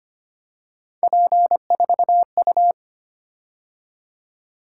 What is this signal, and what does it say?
Morse code, heard as a keyed tone, about 25 wpm, 700 Hz. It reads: P4U
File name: Three Letter-Number Combinations - 25wpm